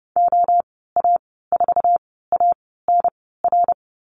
Morse code, heard as a keyed tone, about 30 wpm, spaced as for 17 wpm, 700 Hz. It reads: OA4ANR
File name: Lesson 15g - Number 5 - Call Signs 30wpm